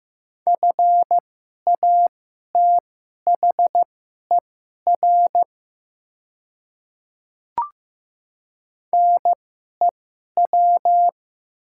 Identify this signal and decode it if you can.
Morse code, heard as a keyed tone, about 15 wpm, spaced as for 11 wpm, 700 Hz. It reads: FATHER NEW
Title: 2x Character Spacing - Top 200 Words - 15wpm